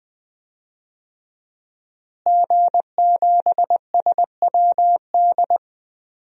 Morse code, heard as a keyed tone, about 20 wpm, 700 Hz. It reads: G7SWD
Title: DXCC-100 Call Signs - 5 Characters - Repeated 2x - 20wpm